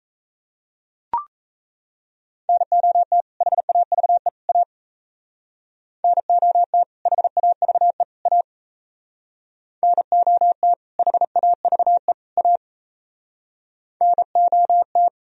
Morse code, heard as a keyed tone, about 33 wpm, 700 Hz. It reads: NOT HAVE A NOT HAVE A NOT HAVE A NOT
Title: Speed-Racing - Sets of 3 Words 28wpm